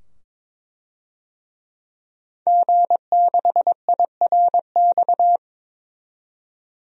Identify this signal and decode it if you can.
Morse code, heard as a keyed tone, about 22 wpm, 700 Hz. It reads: G6IRX
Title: DXCC-100 Call Signs - 5 Characters - 22wpm